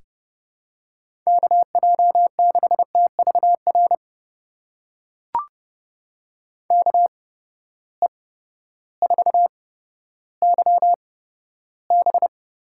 Morse code, heard as a keyed tone, about 29 wpm, spaced as for 14 wpm, 700 Hz. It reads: KJ6TVR KE4YB
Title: USA Call Signs - 6 Characters - 8x Character Spacing - 30wpm